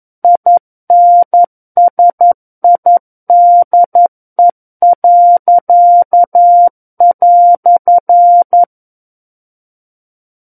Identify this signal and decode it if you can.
Morse code, heard as a keyed tone, about 11 wpm, 700 Hz. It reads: INSIDE."
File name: Book One / 02-Chapter 2 - 11 WPM